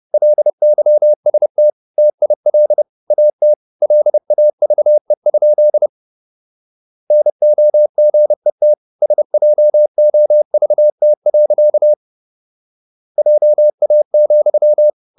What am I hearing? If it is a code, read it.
Morse code, heard as a keyed tone, about 30 wpm, 600 Hz. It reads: LYST TIL AT LAVE? NOGET SJOVT. JA,
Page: Ferie 30 wpm